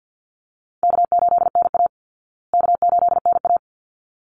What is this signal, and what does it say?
Morse code, heard as a keyed tone, about 49 wpm, 700 Hz. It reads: K8NA K8NA
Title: Easy Call Signs Repeated 2x - 50wpm